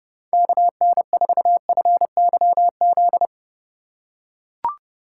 Morse code, heard as a keyed tone, about 30 wpm, 700 Hz. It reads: KN4FYZ
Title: USA Call Signs - 6 Characters - 3x Character Spacing - 30wpm